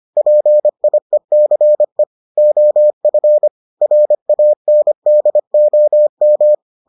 Morse code, heard as a keyed tone, about 25 wpm, 600 Hz. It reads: PIECE OF RANDOM